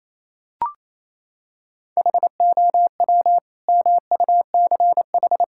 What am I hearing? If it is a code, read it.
Morse code, heard as a keyed tone, about 28 wpm, 700 Hz. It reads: HOW MUCH
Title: Sentences from Top 1000 Words - Review Part 6 - 28wpm